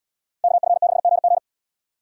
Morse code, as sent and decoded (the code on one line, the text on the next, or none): -.- ..-. -.... --. --.
KF6GG